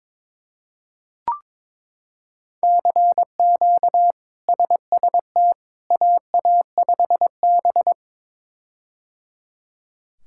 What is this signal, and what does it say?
Morse code, heard as a keyed tone, about 22 wpm, 700 Hz. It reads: CQ SST AA5B
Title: K1USN SST QSO Practice - 22wpm